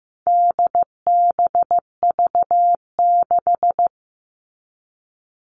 Morse code, heard as a keyed tone, about 15 wpm, 700 Hz. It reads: DBV6
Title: All Cut Numbers - 4-Digit 15wpm